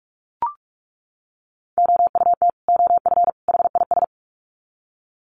Morse code, heard as a keyed tone, about 44 wpm, 700 Hz. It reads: OUT OF HIS
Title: Sets of 3 Words 45wpm